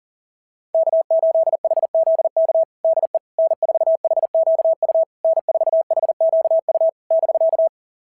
Morse code, heard as a keyed tone, about 38 wpm, 660 Hz. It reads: K8HZK DE N4HQU N4HQU <BK>
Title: SOTA QSO Practice 40wpm